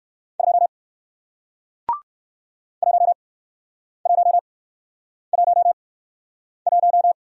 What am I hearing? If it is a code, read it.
Morse code, heard as a keyed tone, about 60 wpm, spaced as for 9 wpm, 700 Hz. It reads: 2 1111